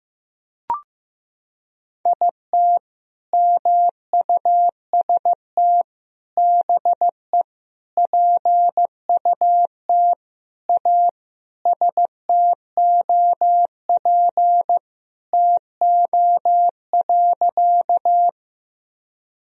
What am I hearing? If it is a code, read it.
Morse code, heard as a keyed tone, about 15 wpm, 700 Hz. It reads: IT MUST BE PUT A STOP TO.
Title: Sentences from Top 1000 Words - Review Part 1 - 15wpm